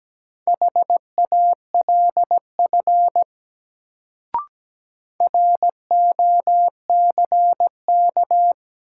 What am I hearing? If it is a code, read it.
Morse code, heard as a keyed tone, about 17 wpm, 700 Hz. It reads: HALF ROCK
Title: Sentences from Top 400 Words - New Word Drill - 17wpm